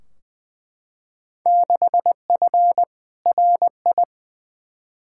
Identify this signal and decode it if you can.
Morse code, heard as a keyed tone, about 21 wpm, 700 Hz. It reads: 6F RI
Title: Field Day Exchanges Encore - 20wpm